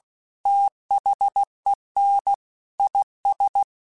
Morse code, heard as a keyed tone, about 16 wpm, 800 Hz. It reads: THEN IS